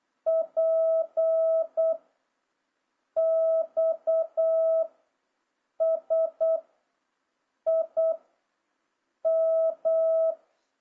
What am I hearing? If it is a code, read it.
Morse code, heard as a keyed tone, about 8 wpm, 650 Hz. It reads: P X S I M
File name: jh8cbh / 2026cw